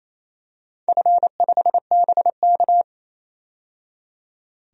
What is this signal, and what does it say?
Morse code, heard as a keyed tone, about 27 wpm, 690 Hz. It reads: F5BK
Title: DXCC-100 Call Signs - 4 Characters - 28wpm